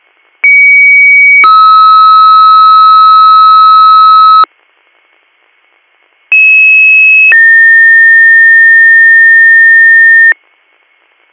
Two-tone page, heard as a one-second tone, then a three-second tone, about 2250 then 1290 Hz, about 2600 then 1800 Hz. It is heard from 0.4 to 10.3 s.